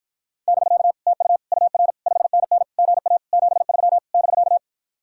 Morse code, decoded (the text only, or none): <BK> TU UR 5NN GA 73 <BK>